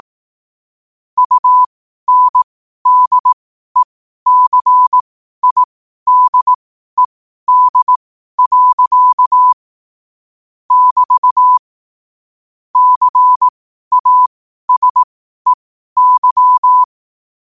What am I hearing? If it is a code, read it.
Morse code, heard as a keyed tone, about 18 wpm, spaced as for 12 wpm, 1000 Hz. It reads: UNDECIDED. <BT> CASEY
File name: qotdfiles / 12wpm